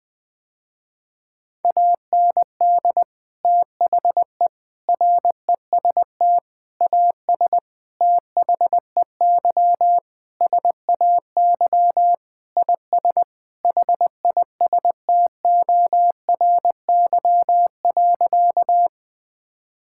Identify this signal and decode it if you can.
Morse code, heard as a keyed tone, about 20 wpm, 700 Hz. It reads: AND THE REST AS THEY SAY IS HISTORY.
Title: Sentences with 8-Word Phrases - Spoken Beginning - 20wpm